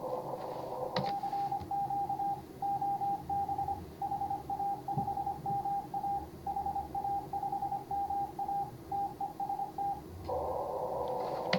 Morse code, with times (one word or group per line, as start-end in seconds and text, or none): 1.0-2.3 CQ
2.6-3.7 OB
4.0-6.2 HA5MA
6.5-8.6 HA5MA
8.9-9.9 TEST